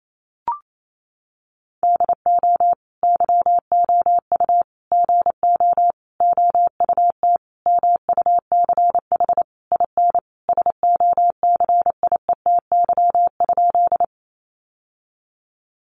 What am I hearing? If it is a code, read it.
Morse code, heard as a keyed tone, about 28 wpm, 700 Hz. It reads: DO YOU GO OUT MUCH IN SOCIETY?
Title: Sentences from Top 700 Words 28wpm